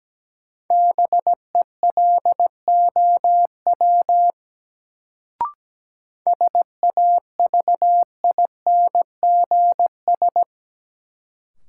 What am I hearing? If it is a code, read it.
Morse code, heard as a keyed tone, about 17 wpm, 700 Hz. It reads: BELOW SAVINGS
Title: Sentences from Top 1200 Words - New Word Drill Alt - 17wpm